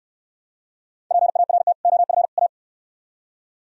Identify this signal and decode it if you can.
Morse code, heard as a keyed tone, about 67 wpm, 700 Hz. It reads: PART OF A